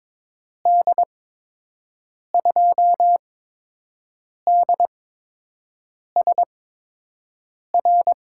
Morse code, heard as a keyed tone, about 22 wpm, spaced as for 6 wpm, 700 Hz. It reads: D2DSR
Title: USA Call Signs - 6 Characters - 8x Character Spacing - 22wpm